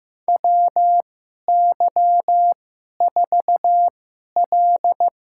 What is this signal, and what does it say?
Morse code, heard as a keyed tone, about 15 wpm, spaced as for 11 wpm, 700 Hz. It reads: WY4L
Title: USA Call Signs - 4 Characters - 2x Character Spacing - 15wpm